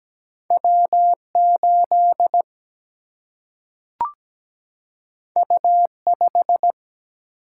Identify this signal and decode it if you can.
Morse code, heard as a keyed tone, about 17 wpm, 700 Hz. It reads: W 8 U 5
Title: Single-Letter Single-Number 17wpm